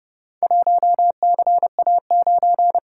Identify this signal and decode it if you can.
Morse code, heard as a keyed tone, about 29 wpm, 700 Hz. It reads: 1CA9